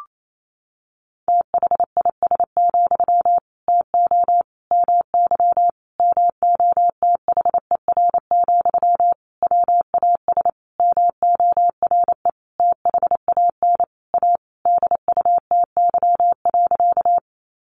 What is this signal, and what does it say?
Morse code, heard as a keyed tone, about 28 wpm, 700 Hz. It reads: THIS, TO MY MOTHER, WAS MORE THAN A DUTY.
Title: Sentences from Top 1500 Words 28wpm